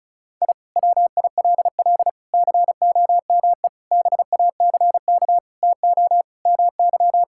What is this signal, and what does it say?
Morse code, heard as a keyed tone, about 36 wpm, 690 Hz. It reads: I WILL COME BACK TO MY
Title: Sentences from Top 1000 Words - Review Part 6 - 35wpm